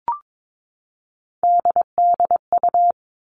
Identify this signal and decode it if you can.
Morse code, heard as a keyed tone, about 22 wpm, 700 Hz. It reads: DDU
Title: All Cut Numbers - 3-Digit 22wpm